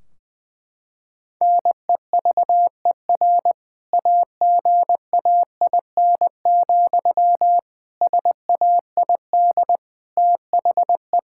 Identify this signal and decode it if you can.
Morse code, heard as a keyed tone, about 20 wpm, 700 Hz. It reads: NEVER AGAIN, SAID THE
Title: Sentences from Top 1600 Words 20wpm